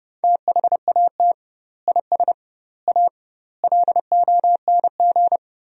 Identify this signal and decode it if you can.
Morse code, heard as a keyed tone, about 32 wpm, 700 Hz. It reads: THAT IS A LONG
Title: 2x Word-Spacing - Sentences from Top 100 Words - 30wpm